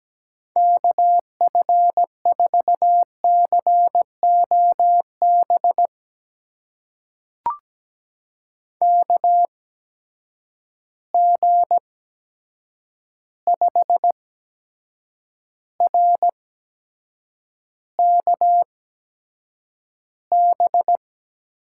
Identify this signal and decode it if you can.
Morse code, heard as a keyed tone, about 17 wpm, spaced as for 7 wpm, 700 Hz. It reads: KF4COB KG5RKB
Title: USA Call Signs - 6 Characters - 8x Character Spacing - 17wpm